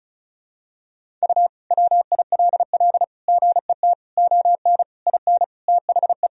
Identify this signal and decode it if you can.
Morse code, heard as a keyed tone, about 36 wpm, 700 Hz. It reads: U WILL GET ON IN THE
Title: Sentences from Top 200 Words and Abbreviations 35wpm